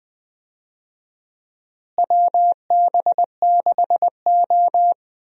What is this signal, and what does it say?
Morse code, heard as a keyed tone, about 20 wpm, 700 Hz. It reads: WB6O
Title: USA Call Signs - 4 Characters - Repeated 2x - 20wpm